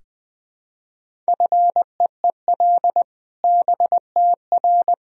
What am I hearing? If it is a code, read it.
Morse code, heard as a keyed tone, about 20 wpm, 700 Hz. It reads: FEEL BTR
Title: Sentences from Top 400 Words and Abbreviations 20wpm